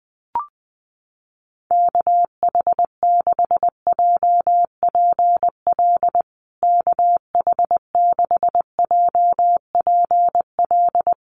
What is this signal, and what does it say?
Morse code, heard as a keyed tone, about 20 wpm, 700 Hz. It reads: KH6JPL KH6JPL
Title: USA Call Signs Repeated 3x 20wpm